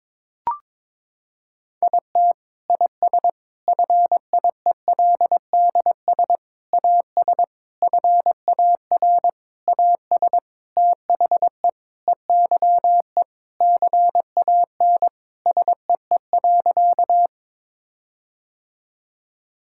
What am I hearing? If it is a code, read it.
Morse code, heard as a keyed tone, about 22 wpm, 700 Hz. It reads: IT IS FIELDS AS FAR AS THE EYE CAN SEE.
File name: Sentences with 7-Word Phrases - 22wpm